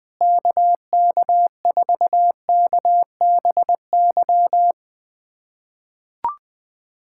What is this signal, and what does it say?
Morse code, heard as a keyed tone, about 20 wpm, 700 Hz. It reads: KK4KBY
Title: USA Callsign DE Callsign - 6 Characters - 20wpm